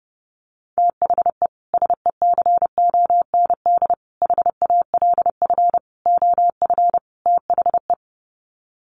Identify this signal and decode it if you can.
Morse code, heard as a keyed tone, about 30 wpm, 700 Hz. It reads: THE SECOND HALF OF THE
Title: Sets of 5 Words 30wpm